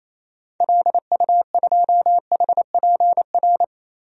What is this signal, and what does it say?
Morse code, heard as a keyed tone, about 28 wpm, 700 Hz. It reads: LU2HPR
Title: DXCC-100 Call Signs - 6 Characters - Repeated 2x - 28wpm